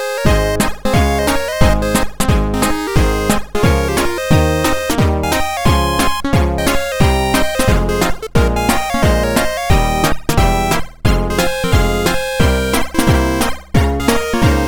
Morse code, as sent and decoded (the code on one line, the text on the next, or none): none